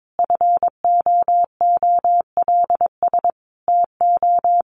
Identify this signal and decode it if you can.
Morse code, heard as a keyed tone, about 22 wpm, 700 Hz. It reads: FOOLS TO
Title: A Sound Of Thunder 22wpm